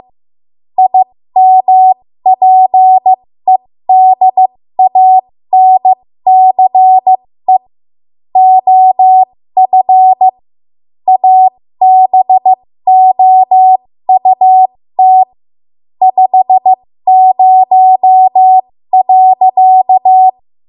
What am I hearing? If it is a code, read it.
Morse code, heard as a keyed tone, about 15 wpm, spaced as for 13 wpm, 750 Hz. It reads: IMPEDANCE OF ABOUT 50.
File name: Morse / 13WPM